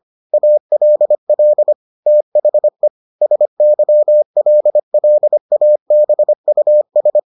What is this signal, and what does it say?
Morse code, heard as a keyed tone, about 25 wpm, 600 Hz. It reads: ALL THE SYLLABUS